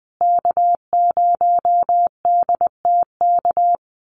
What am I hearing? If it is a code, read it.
Morse code, heard as a keyed tone, about 20 wpm, 700 Hz. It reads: K0DTK
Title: USA Call Signs Repeated 3x 20wpm